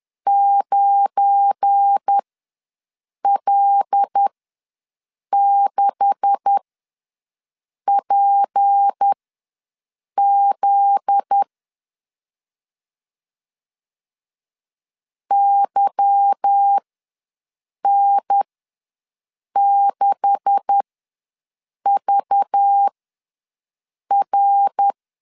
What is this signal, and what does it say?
Morse code, heard as a keyed tone, about 10 wpm, spaced as for 6 wpm, 800 Hz. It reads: 9L6PZ YN6VR
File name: LEZIONI / 9 286 ws 10wpm 4032